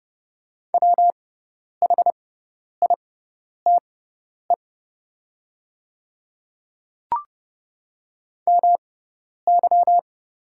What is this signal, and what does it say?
Morse code, heard as a keyed tone, about 30 wpm, spaced as for 10 wpm, 700 Hz. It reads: WHITE MY
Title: 6x Character Spacing - Top 500 Words - 30wpm